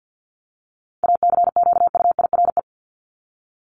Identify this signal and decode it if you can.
Morse code, heard as a keyed tone, about 50 wpm, 700 Hz. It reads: ACQUIRE
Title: Top 1501-1750 Words 50wpm